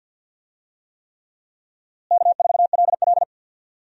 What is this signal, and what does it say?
Morse code, heard as a keyed tone, about 53 wpm, 700 Hz. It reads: K4LL
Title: USA Call Signs - 4 Characters - Repeated 3x - 50wpm